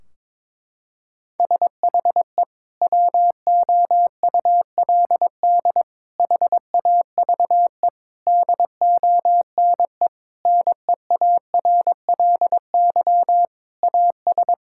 Morse code, heard as a keyed tone, about 21 wpm, 700 Hz. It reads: SHE WOULD HAVE DONE NEARLY AS
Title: Sentences from Top 1600 Words 22wpm